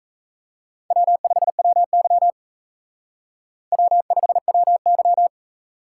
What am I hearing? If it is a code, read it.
Morse code, heard as a keyed tone, about 37 wpm, 700 Hz. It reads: W5WY W5WY